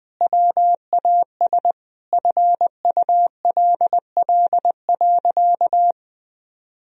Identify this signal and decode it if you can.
Morse code, heard as a keyed tone, about 20 wpm, 700 Hz. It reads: WAS FULL.
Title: Sentences from Top 1000 Words - Review Part 3 - 20wpm